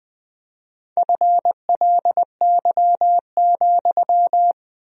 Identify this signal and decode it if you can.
Morse code, heard as a keyed tone, about 20 wpm, 700 Hz. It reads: FLY,
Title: 4x Word-Spacing - Sentences from Top 500 Words - 20wpm